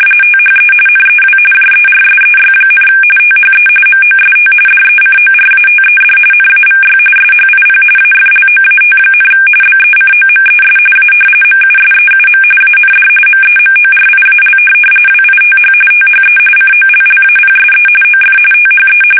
RTTY, about 45.45 baud. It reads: THE NEW JERSEY, THE MOST DECORATED NAVY SHIP AFLOAT, BECOMES A MUSEUM AND MEMORIAL WITH THE COLLAPSE OF THE
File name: fsk850 / battleshipnj